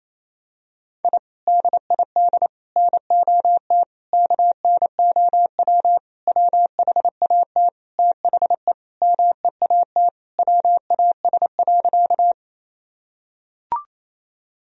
Morse code, heard as a keyed tone, about 28 wpm, 700 Hz. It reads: I DID NOT KNOW WHAT THE MEAT WAS.
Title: Sentences from Top 1000 Words - Review Part 3 - 28wpm